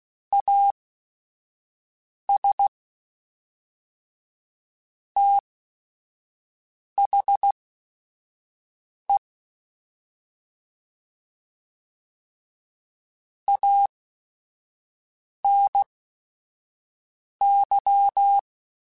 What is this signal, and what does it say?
Morse code, heard as a keyed tone, about 16 wpm, spaced as for 5 wpm, 800 Hz. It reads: AS THE ANY
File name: Technical Subjects / Technical Subjects-mp3s / OR Gate @5